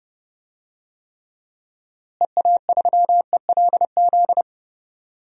Morse code, heard as a keyed tone, about 30 wpm, 700 Hz. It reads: EA3ELZ
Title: DXCC-100 Call Signs - 6 Characters - Repeated 2x - 30wpm